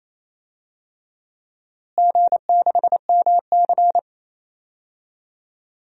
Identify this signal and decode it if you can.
Morse code, heard as a keyed tone, about 28 wpm, 700 Hz. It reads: G6MC
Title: DXCC-100 Call Signs - 4 Characters - Repeated 2x - 28wpm